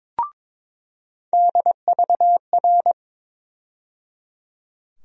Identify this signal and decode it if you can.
Morse code, heard as a keyed tone, about 22 wpm, 700 Hz. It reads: DVR
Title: USA Call Sign Postfixes - 3 Letters 22wpm